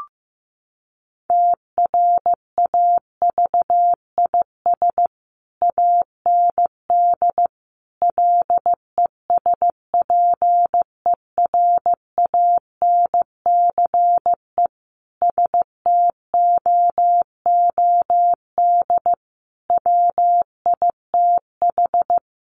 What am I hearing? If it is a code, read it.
Morse code, heard as a keyed tone, about 15 wpm, 700 Hz. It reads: TRAVIS AND LESPERANCE STOOD WITH
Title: A Sound Of Thunder 15wpm